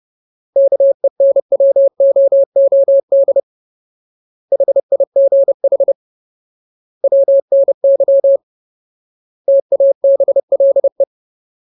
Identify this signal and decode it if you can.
Morse code, heard as a keyed tone, about 30 wpm, 550 Hz. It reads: KENWOOD HIGH WNY TABLE
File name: Ultimate Word List / 30WPM Ultimate